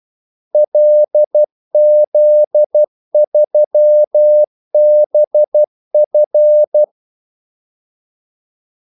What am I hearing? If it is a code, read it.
Morse code, heard as a keyed tone, about 12 wpm, 600 Hz. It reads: LZ3BF
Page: 2 ekstra ordmellemrum mellem kaldesignalerne